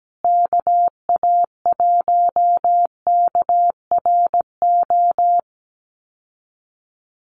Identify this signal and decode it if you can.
Morse code, heard as a keyed tone, about 17 wpm, 700 Hz. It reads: KA1KRO
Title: USA Call Signs Repeated 2x 17wpm